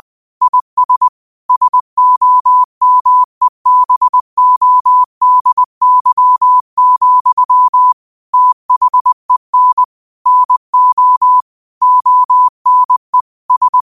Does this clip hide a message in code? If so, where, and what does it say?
Yes: Morse code from 0.4 to 13.8 s, heard as a keyed tone, about 20 wpm, 1000 Hz. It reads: IS SOMEBODY, THEN NO ONE S